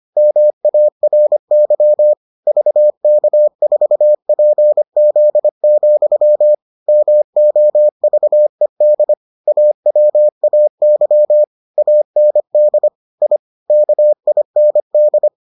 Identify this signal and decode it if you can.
Morse code, heard as a keyed tone, about 25 wpm, 600 Hz. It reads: MARY VK4PZ, MOVED AWAY AND I KIND